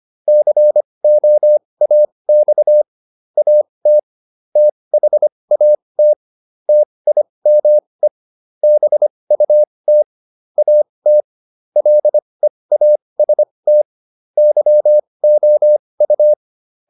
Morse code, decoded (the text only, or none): COAX AT THAT TIME BUT AT LEAST YOU